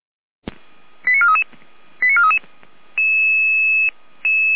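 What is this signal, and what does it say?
Five-tone selective call (ZVEI), heard as quick runs of five tones, tones of about 70 ms. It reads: 89433 89433
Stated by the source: ZVEI five-tone call received by SDR at Feuerwehr Orlamünde fire brigade.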